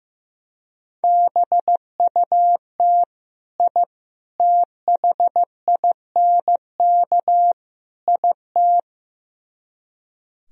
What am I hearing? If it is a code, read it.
Morse code, heard as a keyed tone, about 15 wpm, 700 Hz. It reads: BUT I THINK IT
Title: Speed-Racing - Sets of 4 Words 15wpm